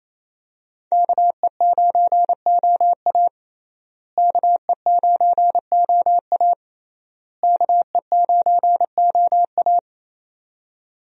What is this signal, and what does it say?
Morse code, heard as a keyed tone, about 27 wpm, 700 Hz. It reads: KE9OA KE9OA KE9OA
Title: USA Call Signs - 5 Characters - Repeated 3x - 28wpm